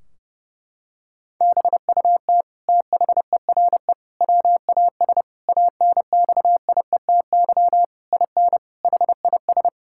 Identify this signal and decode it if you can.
Morse code, heard as a keyed tone, about 30 wpm, 700 Hz. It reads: BUT THERE WAS ANXIETY IN HIS
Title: Sentences from Top 1600 Words 30wpm